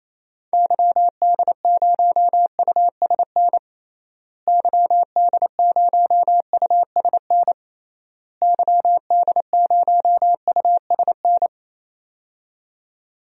Morse code, heard as a keyed tone, about 28 wpm, 700 Hz. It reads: YD0USN YD0USN YD0USN
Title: DXCC-100 Call Signs - 6 Characters - Repeated 3x - 28wpm